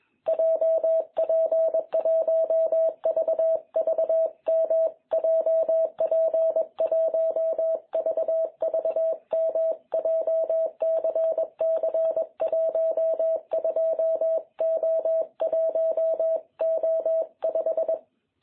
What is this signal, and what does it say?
Morse code, heard as a keyed tone, about 22 wpm, 660 Hz. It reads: JP1VVM JP1VVM JCC12O1O5
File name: jh8cbh / 2021cw